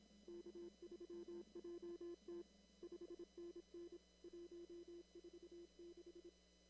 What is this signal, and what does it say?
Morse code, heard as a keyed tone, about 26 wpm, 340 Hz. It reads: K3JT 5NN 146
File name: CW / kt5j wpx cw-op-k5pi